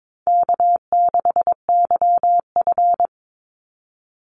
Morse code, heard as a keyed tone, about 22 wpm, 700 Hz. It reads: K6YF
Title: Easy Call Signs Repeated 2x - 22wpm